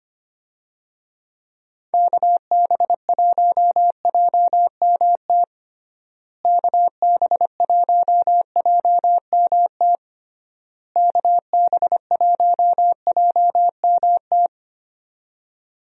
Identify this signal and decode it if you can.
Morse code, heard as a keyed tone, about 25 wpm, 700 Hz. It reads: KB1JMT KB1JMT KB1JMT
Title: USA Call Signs - 6 Characters - Repeated 3x - 25wpm